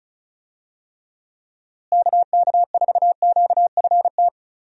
Morse code, heard as a keyed tone, about 36 wpm, 700 Hz. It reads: KK4QFT
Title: USA Call Signs - 6 Characters - Repeated 3x - 35wpm